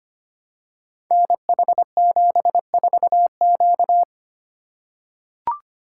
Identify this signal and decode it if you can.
Morse code, heard as a keyed tone, about 25 wpm, 700 Hz. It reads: NH74Q
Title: Code Groups - 5 Letter:Numbers - 25wpm